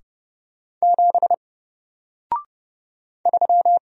Morse code, heard as a keyed tone, about 30 wpm, 700 Hz. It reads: 7 3
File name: Single-Letter-Number-30wpm